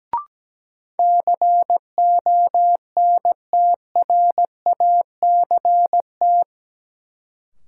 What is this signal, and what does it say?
Morse code, heard as a keyed tone, about 17 wpm, 700 Hz. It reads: CONTRACT
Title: Sentences from Top 1400 Words - New Word Drill - 17wpm